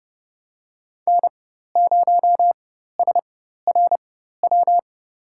Morse code, heard as a keyed tone, about 30 wpm, spaced as for 14 wpm, 700 Hz. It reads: N0SRW
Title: USA Call Signs - 5 Characters - 4x Character Spacing - 30wpm